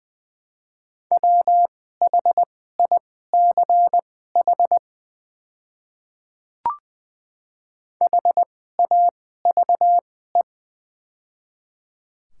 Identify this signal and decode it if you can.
Morse code, heard as a keyed tone, about 20 wpm, spaced as for 14 wpm, 700 Hz. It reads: WHICH HAVE
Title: 2x Character Spacing - Top 100 Words - 20wpm